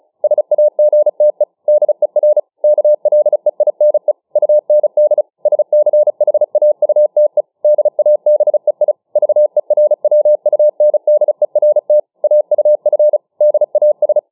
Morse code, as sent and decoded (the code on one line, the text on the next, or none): ... .- --. - . / -.. .-. / -.- .-.. .. -. / ..- -. -.. / ... -.-. .... .- ..- - . / -.. .- -... .. / ...- . .-. .-- ..- -. -.. .-. - / .- ..- ..-. / -.. .- ...
SAGTE DER KLEINE UND SCHAUTE DABEI VERWUNDERT AUF DAS